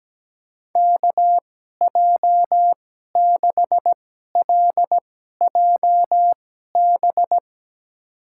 Morse code, heard as a keyed tone, about 17 wpm, spaced as for 12 wpm, 700 Hz. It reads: KJ6LJB
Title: USA Call Signs - 6 Characters - 2x Character Spacing - 17wpm